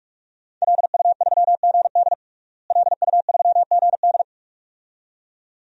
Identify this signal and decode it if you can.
Morse code, heard as a keyed tone, about 43 wpm, 700 Hz. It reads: LU3GD LU3GD
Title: DXCC-100 Call Signs - 5 Characters - Repeated 2x - 45wpm